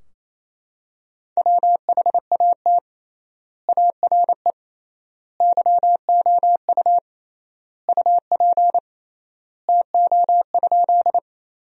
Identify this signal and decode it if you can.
Morse code, heard as a keyed tone, about 28 wpm, 700 Hz. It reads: WHAT ARE YOU UP TO?
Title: 3x Word-Spacing - Sentences from Top 100 Words - 28wpm